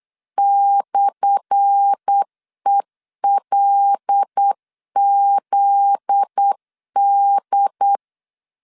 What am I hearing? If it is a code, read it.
Morse code, heard as a keyed tone, about 8 wpm, 800 Hz. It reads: /ELZD
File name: LEZIONI / 10 SIGNS / 10B INT 8wpm 3534